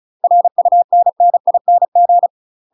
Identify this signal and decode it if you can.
Morse code, heard as a keyed tone, about 36 wpm, 700 Hz. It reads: RUNNING
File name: UK MASTER QSOs TRAINING / 35WPM / UK MASTER QSOs 0020